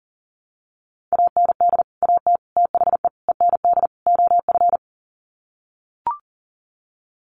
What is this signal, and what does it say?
Morse code, heard as a keyed tone, about 40 wpm, 700 Hz. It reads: AND AT THE END OF
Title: Sets of 5 Words 40wpm